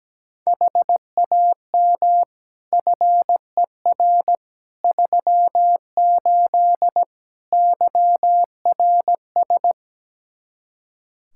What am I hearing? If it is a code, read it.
Morse code, heard as a keyed tone, about 17 wpm, 700 Hz. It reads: HAM FER 38 YRS
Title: K1USN SST QSO Practice 17wpm